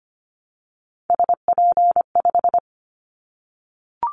Morse code, heard as a keyed tone, about 25 wpm, 700 Hz. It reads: SP5
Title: Callsign Prefixes for Top 25 DXCC - 25wpm